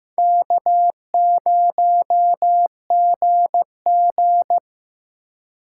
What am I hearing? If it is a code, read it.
Morse code, heard as a keyed tone, about 15 wpm, 700 Hz. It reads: K0GG
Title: USA Call Signs - 4 Characters - 3x Character Spacing - 15wpm